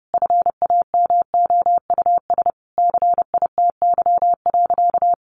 Morse code, heard as a keyed tone, about 30 wpm, 700 Hz. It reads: FAMOUS CITY.
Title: Sentences from Top 1500 Words 30wpm